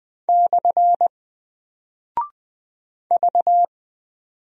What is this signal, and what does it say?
Morse code, heard as a keyed tone, about 20 wpm, 700 Hz. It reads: / V
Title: Single Character (Letter:Number:Punctuation:Prosign) Alt 20wpm